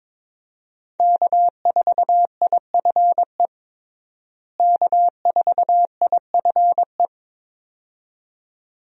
Morse code, heard as a keyed tone, about 22 wpm, 700 Hz. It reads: K4IFE K4IFE
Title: USA Call Signs - 5 Characters - Repeated 2x - 22wpm